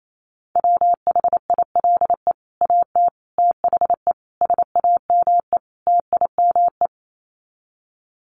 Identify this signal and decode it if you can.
Morse code, heard as a keyed tone, about 28 wpm, 700 Hz. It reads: WHILE AT THE SAME TIME
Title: Sets of 5 Words 28wpm